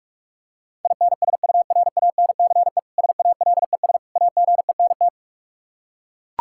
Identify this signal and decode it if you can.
Morse code, heard as a keyed tone, about 41 wpm, 700 Hz. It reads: INSURANCE SALES AGENT
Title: K1USN SST QSO Practice 45wpm